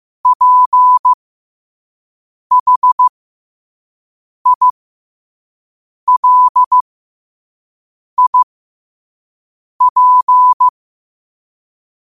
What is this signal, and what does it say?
Morse code, heard as a keyed tone, about 15 wpm, spaced as for 5 wpm, 1000 Hz. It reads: PHILIP